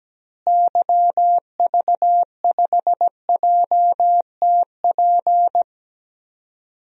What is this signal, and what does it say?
Morse code, heard as a keyed tone, about 17 wpm, 700 Hz. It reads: YV5JTP
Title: DXCC-100 Call Signs - 6 Characters - Repeated 3x - 17wpm